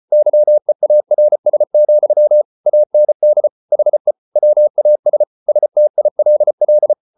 Morse code, heard as a keyed tone, about 34 wpm, 600 Hz. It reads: YEARS, AND HE WAS STILL